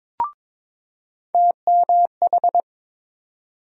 Morse code, heard as a keyed tone, about 21 wpm, 700 Hz. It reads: TMH
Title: USA Call Sign Postfixes - 3 Letters 22wpm